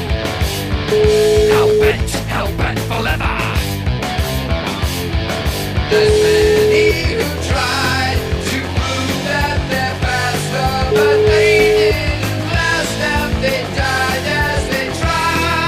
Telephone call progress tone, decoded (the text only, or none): ringback tone